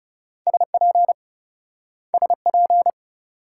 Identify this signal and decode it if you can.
Morse code, heard as a keyed tone, about 38 wpm, 700 Hz. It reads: SP SP